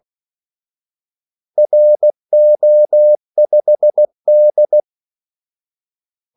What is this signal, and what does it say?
Morse code, heard as a keyed tone, about 16 wpm, 600 Hz. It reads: RO5D
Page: Kaldesignaler 16 wpm